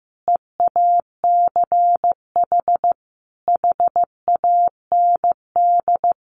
Morse code, heard as a keyed tone, about 15 wpm, 700 Hz. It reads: EACH HAND
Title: Sentences from Top 1400 Words 15wpm